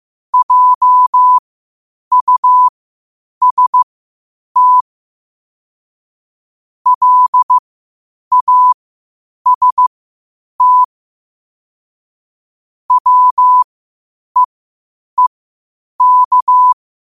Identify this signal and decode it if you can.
Morse code, heard as a keyed tone, about 15 wpm, spaced as for 8 wpm, 1000 Hz. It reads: JUST LAST WEEK
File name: qotdfiles / 8wpm